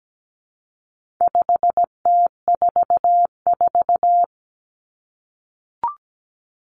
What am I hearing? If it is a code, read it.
Morse code, heard as a keyed tone, about 17 wpm, 700 Hz. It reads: 5T44
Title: Common Cut Numbers - 4-Digit 17wpm